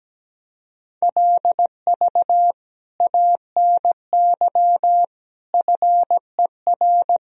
Morse code, heard as a keyed tone, about 17 wpm, 700 Hz. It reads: LV ANY FER
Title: Sentences from Top 400 Words and Abbreviations 17wpm